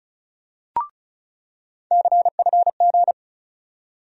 Morse code, heard as a keyed tone, about 36 wpm, 700 Hz. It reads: CFG
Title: USA Call Sign Postfixes - 3 Letters 35wpm